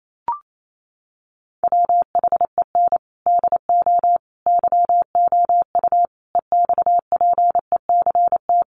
Morse code, heard as a keyed tone, about 28 wpm, 700 Hz. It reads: WHEN DO YOU EXPECT
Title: Sentences from Top 1100 Words 28wpm